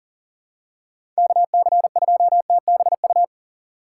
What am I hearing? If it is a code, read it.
Morse code, heard as a keyed tone, about 38 wpm, 700 Hz. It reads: KC2TBU
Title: USA Call Signs - 6 Characters - 2x Character Spacing - 40wpm